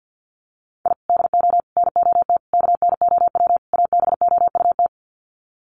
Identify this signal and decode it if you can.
Morse code, heard as a keyed tone, about 52 wpm, 700 Hz. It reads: I DO NOT KNOW ABOUT
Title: Sets of 5 Words 50wpm